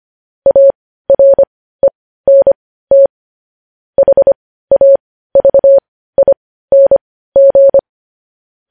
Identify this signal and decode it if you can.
Morse code, heard as a keyed tone, about 25 wpm, spaced as for 15 wpm, 550 Hz. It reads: ARENT HAVING